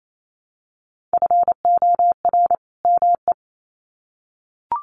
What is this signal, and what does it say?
Morse code, heard as a keyed tone, about 28 wpm, 700 Hz. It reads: FOR ME
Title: Sets of 2 Words 28wpm